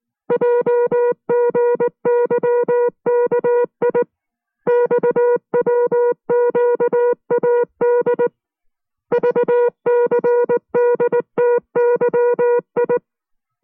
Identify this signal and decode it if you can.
Morse code, heard as a keyed tone, about 19 wpm, 480 Hz. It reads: JGYKI XWQAD VCDTYI